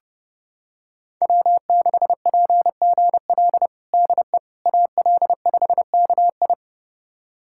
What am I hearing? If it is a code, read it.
Morse code, heard as a keyed tone, about 30 wpm, 700 Hz. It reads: W6PGL DE AL5KI
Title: USA Callsign DE Callsign - 5 Characters - 30wpm